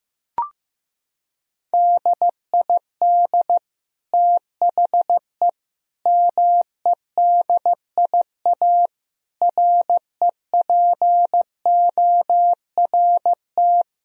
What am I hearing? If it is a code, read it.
Morse code, heard as a keyed tone, about 16 wpm, 700 Hz. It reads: DID THE MEDIA REPORT
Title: Sentences from Top 1000 Words - Review Part 4 - 15wpm